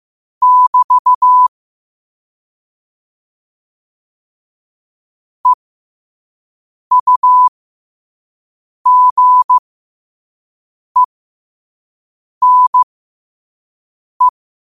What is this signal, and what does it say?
Morse code, heard as a keyed tone, about 15 wpm, spaced as for 5 wpm, 1000 Hz. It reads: <BT> EUGENE
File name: 5wpm